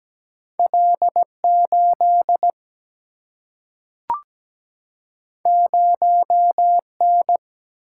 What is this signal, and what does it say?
Morse code, heard as a keyed tone, about 17 wpm, 700 Hz. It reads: L8 0N
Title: Single-Letter Single-Number 17wpm